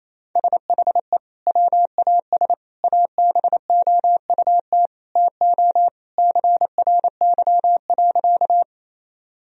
Morse code, heard as a keyed tone, about 28 wpm, 700 Hz. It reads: SHE WAS ABOUT TO CRY.
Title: Sentences from Top 1000 Words - Review Part 2 - 28wpm